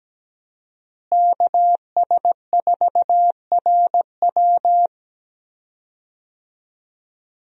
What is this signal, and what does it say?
Morse code, heard as a keyed tone, about 17 wpm, 700 Hz. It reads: KS4RW
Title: USA Call Signs - 5 Characters - Repeated 3x - 17wpm